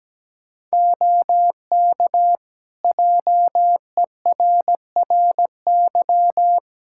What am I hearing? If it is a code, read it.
Morse code, heard as a keyed tone, about 17 wpm, 700 Hz. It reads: OK JERRY
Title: Practice QSO - Exchange 3 - More Info - 17wpm